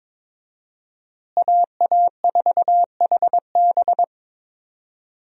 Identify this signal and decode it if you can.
Morse code, heard as a keyed tone, about 22 wpm, 700 Hz. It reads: AA4HB
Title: USA Call Signs - 5 Characters - Repeated 2x - 22wpm